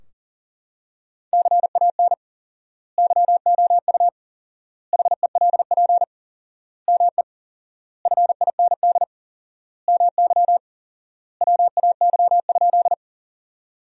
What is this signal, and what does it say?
Morse code, heard as a keyed tone, about 40 wpm, 700 Hz. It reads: CAN YOU HELP ME FIND MY WAY?
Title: 4x Word-Spacing - Sentences from Top 200 Words - 40wpm